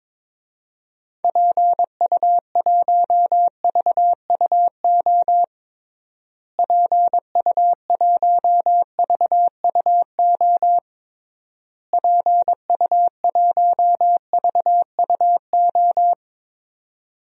Morse code, heard as a keyed tone, about 22 wpm, 700 Hz. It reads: PU1VUO PU1VUO PU1VUO
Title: DXCC-100 Call Signs - 6 Characters - Repeated 3x - 22wpm